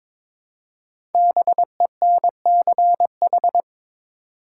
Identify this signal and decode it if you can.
Morse code, heard as a keyed tone, about 22 wpm, 700 Hz. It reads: BENCH
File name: 5 Letter Words - First 3 Letters Given - 22wpm